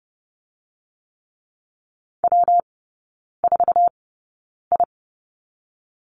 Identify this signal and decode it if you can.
Morse code, heard as a keyed tone, about 30 wpm, 700 Hz. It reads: W 4 I
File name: Lesson 19e - Letter w - Three Characters 30wpm